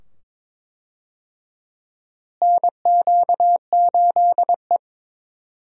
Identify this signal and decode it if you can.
Morse code, heard as a keyed tone, about 22 wpm, 700 Hz. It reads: NQ8E
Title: USA Call Signs - 4 Characters - Repeated 2x - 22wpm